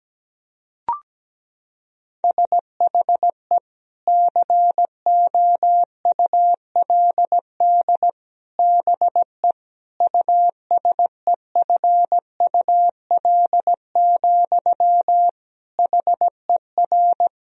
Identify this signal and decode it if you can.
Morse code, heard as a keyed tone, about 17 wpm, 700 Hz. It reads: SHE COULD BE USEFUL, HER
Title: Sentences from Top 1000 Words - Review Part 4 - 17wpm